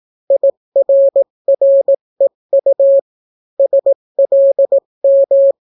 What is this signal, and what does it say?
Morse code, heard as a keyed tone, about 18 wpm, 550 Hz. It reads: IRREU SLM